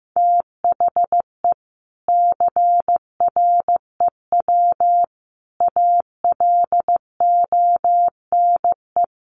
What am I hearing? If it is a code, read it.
Morse code, heard as a keyed tone, about 15 wpm, 700 Hz. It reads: THE CREW ALONE
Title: Sentences from Top 1500 Words 15wpm